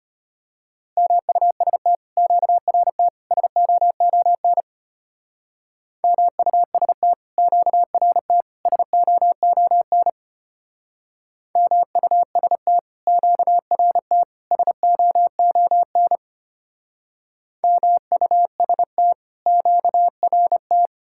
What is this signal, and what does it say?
Morse code, heard as a keyed tone, about 33 wpm, 700 Hz. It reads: MUST QRT SOON MUST QRT SOON MUST QRT SOON MUST QRT